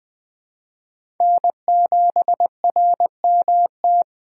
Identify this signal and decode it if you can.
Morse code, heard as a keyed tone, about 20 wpm, 700 Hz. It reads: N7RMT
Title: USA Call Signs - 5 Characters - 3x Character Spacing - 20wpm